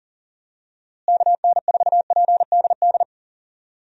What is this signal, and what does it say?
Morse code, heard as a keyed tone, about 40 wpm, 700 Hz. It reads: KN4PDD
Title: USA Call Signs - 6 Characters - Repeated 2x - 40wpm